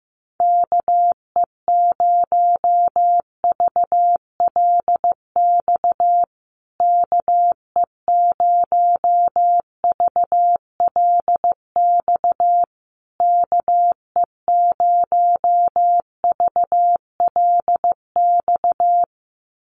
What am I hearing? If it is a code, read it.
Morse code, heard as a keyed tone, about 15 wpm, 700 Hz. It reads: KE0VLX KE0VLX KE0VLX
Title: USA Call Signs Repeated 3x 15wpm